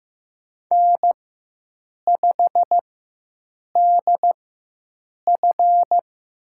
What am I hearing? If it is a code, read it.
Morse code, heard as a keyed tone, about 15 wpm, spaced as for 7 wpm, 700 Hz. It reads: N5DF
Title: USA Call Signs - 4x Character Spacing - 15wpm